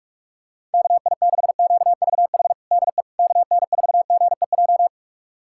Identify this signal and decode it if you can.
Morse code, heard as a keyed tone, about 41 wpm, 700 Hz. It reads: KI6QVH DE KN4GEJ
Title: USA Callsign DE Callsign 45wpm